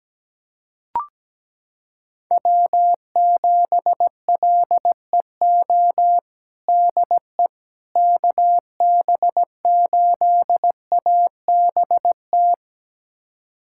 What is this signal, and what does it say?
Morse code, heard as a keyed tone, about 17 wpm, 700 Hz. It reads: W7LEO DE KB8ABT
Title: USA Callsign DE Callsign 17wpm